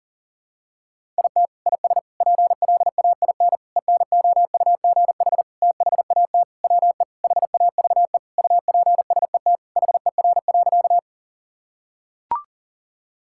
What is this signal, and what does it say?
Morse code, heard as a keyed tone, about 40 wpm, 700 Hz. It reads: IT IS PLAIN ENOUGH THAT WE HAVE UPSET HER.
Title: Sentences from Top 1000 Words - Review Part 5 - 40wpm